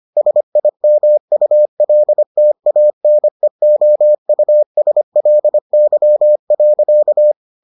Morse code, heard as a keyed tone, about 25 wpm, 600 Hz. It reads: SIMULTANEOUSLY.